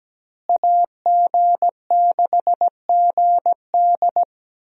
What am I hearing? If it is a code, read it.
Morse code, heard as a keyed tone, about 17 wpm, 700 Hz. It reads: AG6GD
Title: USA Call Signs - 5 Characters - Repeated 2x - 17wpm